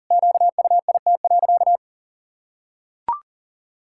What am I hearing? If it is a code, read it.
Morse code, heard as a keyed tone, about 37 wpm, 700 Hz. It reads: QUIT.
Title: Pangram Challenge v2 - 40wpm